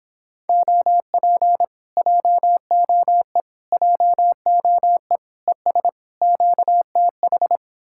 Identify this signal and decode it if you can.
Morse code, heard as a keyed tone, about 29 wpm, 700 Hz. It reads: OP JOE JOE ES QTH